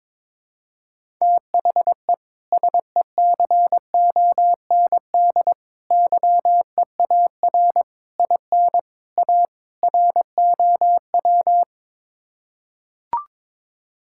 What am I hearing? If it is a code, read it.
Morse code, heard as a keyed tone, about 22 wpm, 700 Hz. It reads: THE SECOND YEAR IN A ROW
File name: practice / 22wpm / Sets of 6 Words - 22wpm